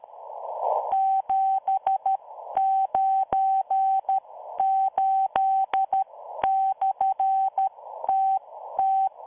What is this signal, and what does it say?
Morse code, heard as a keyed tone, about 13 wpm, spaced as for 11 wpm, 770 Hz. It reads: 798/TT